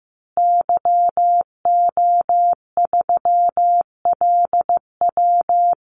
Morse code, heard as a keyed tone, about 15 wpm, 700 Hz. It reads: YO3LW
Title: Call Signs Repeated 2x - 15wpm